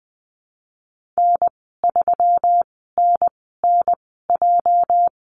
Morse code, heard as a keyed tone, about 20 wpm, spaced as for 15 wpm, 700 Hz. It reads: N3NNJ
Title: 238 Lesson 38g - Letter q - Call Signs 20wpm